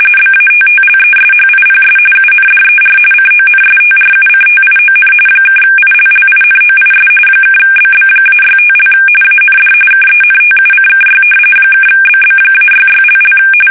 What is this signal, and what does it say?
RTTY, about 45.45 baud. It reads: BATTLESHIPS UNDER CONSTRUCTION (ILLINOIS (BB-65) AND KENTUCKY (BB-66))
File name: fsk850 / battleshipnj